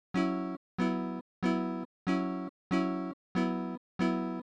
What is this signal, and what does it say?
There is no code.